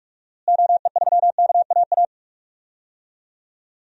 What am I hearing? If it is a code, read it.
Morse code, heard as a keyed tone, about 44 wpm, 700 Hz. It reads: OE3KAA
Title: DXCC-100 Call Signs - 6 Characters - Repeated 2x - 45wpm